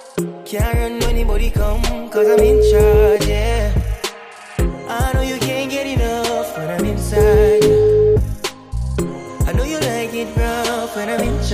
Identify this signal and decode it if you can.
Ringback tone (Europe, 425 Hz), with noise.